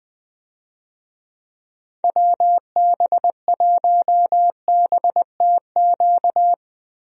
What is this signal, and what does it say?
Morse code, heard as a keyed tone, about 20 wpm, 700 Hz. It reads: WB1BTQ
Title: USA Call Signs - 6 Characters - Repeated 3x - 20wpm